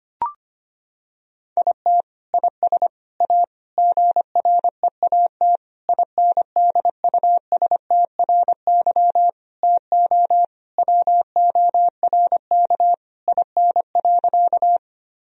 Morse code, heard as a keyed tone, about 25 wpm, 700 Hz. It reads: IT IS A GREAT INDUSTRY TO WORK IN.
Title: Sentences from Top 1000 Words - Review Part 4 - 25wpm